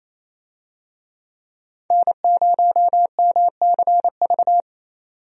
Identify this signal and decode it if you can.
Morse code, heard as a keyed tone, about 28 wpm, 700 Hz. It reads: N0MCV
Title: USA Call Signs - 5 Characters - 28wpm